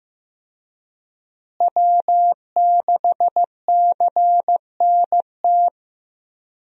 Morse code, heard as a keyed tone, about 15 wpm, 700 Hz. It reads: W6CNT
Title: USA Call Signs - 5 Characters - Repeated 2x - 15wpm